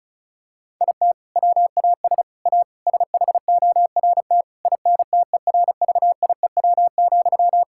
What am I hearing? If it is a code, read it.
Morse code, heard as a keyed tone, about 36 wpm, 700 Hz. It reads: IT WAS A SHORT INTERVIEW,
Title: Sentences from Top 1600 Words 35wpm